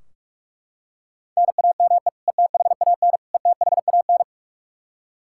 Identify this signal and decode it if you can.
Morse code, heard as a keyed tone, about 41 wpm, 700 Hz. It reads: NAME ETHAN ETHAN
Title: Practice QSO - Exchange 1 - Beginning - 45wpm